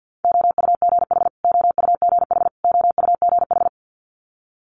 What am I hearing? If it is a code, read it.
Morse code, heard as a keyed tone, about 50 wpm, 700 Hz. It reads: OUGH OUGH OUGH
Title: Multi-letter Phonemes Repeated 3x 50wpm